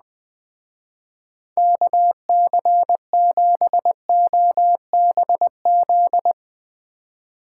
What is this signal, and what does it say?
Morse code, heard as a keyed tone, about 20 wpm, 700 Hz. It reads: KC7OBZ
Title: Speed Racing - USA 2x3 Call Signs 17wpm